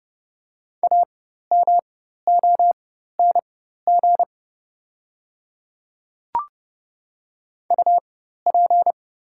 Morse code, heard as a keyed tone, about 30 wpm, spaced as for 14 wpm, 690 Hz. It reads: AMONG UP
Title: 4x Character Spacing - Top 500 Words - 30wpm